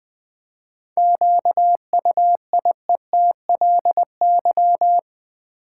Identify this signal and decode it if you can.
Morse code, heard as a keyed tone, about 20 wpm, 700 Hz. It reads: QUIETLY
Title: 7 Letter Words - First 3 Letters Given - 20wpm